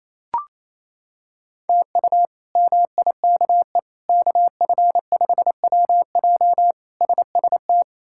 Morse code, heard as a keyed tone, about 28 wpm, 700 Hz. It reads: TU MIKE KF5WJ SST
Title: K1USN SST QSO Practice - 28wpm